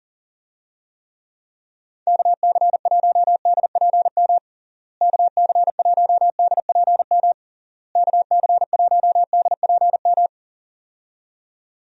Morse code, heard as a keyed tone, about 40 wpm, 700 Hz. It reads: KC1DPM KC1DPM KC1DPM
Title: USA Call Signs - 6 Characters - Repeated 3x - 40wpm